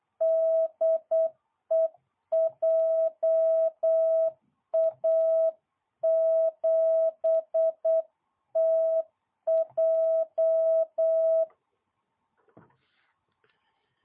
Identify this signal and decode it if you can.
Morse code, heard as a keyed tone, about 8 wpm, 660 Hz. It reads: DE JA7TJ